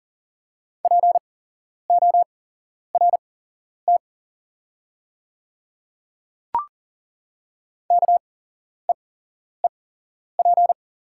Morse code, heard as a keyed tone, about 40 wpm, spaced as for 11 wpm, 700 Hz. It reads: PORT KEEP
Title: 8x Character Spacing - Top 500 Words - 40wpm